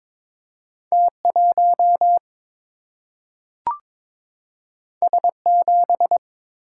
Morse code, heard as a keyed tone, about 22 wpm, 700 Hz. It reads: T1 S7
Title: Single-Letter Single-Number 22wpm